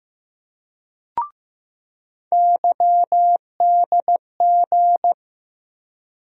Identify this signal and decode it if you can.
Morse code, heard as a keyed tone, about 15 wpm, 700 Hz. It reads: YDG
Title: USA Call Sign Postfixes - 3 Letters 15wpm